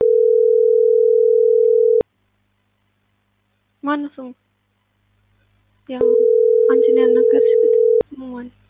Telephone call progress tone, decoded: ringback tone

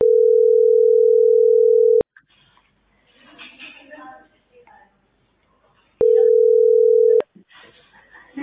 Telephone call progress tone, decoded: ringback tone

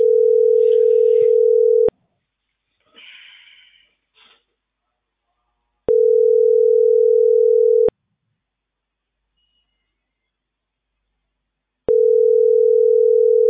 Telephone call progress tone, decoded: ringback tone

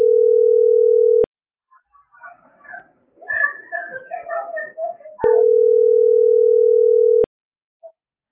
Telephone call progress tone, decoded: ringback tone

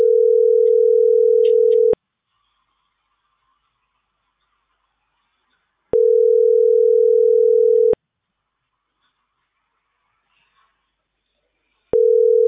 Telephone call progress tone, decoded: ringback tone

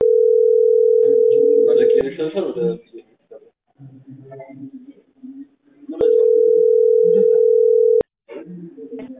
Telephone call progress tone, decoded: ringback tone

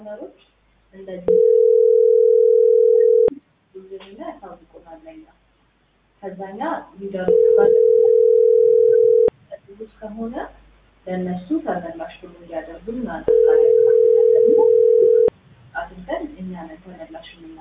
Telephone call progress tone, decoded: ringback tone

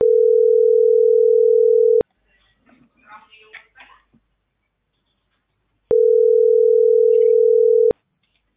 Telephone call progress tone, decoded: ringback tone

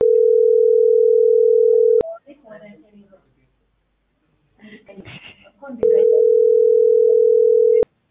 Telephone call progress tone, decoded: ringback tone

